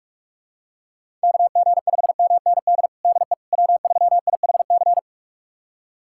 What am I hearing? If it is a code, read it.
Morse code, heard as a keyed tone, about 45 wpm, 700 Hz. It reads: KG5MND DE W3IHC
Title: USA Callsign DE Callsign 45wpm